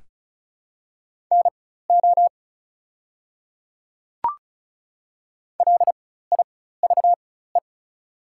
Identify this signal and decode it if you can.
Morse code, heard as a keyed tone, about 33 wpm, spaced as for 16 wpm, 690 Hz. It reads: NO LIVE
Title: 4x Character Spacing - Top 200 Words - 35wpm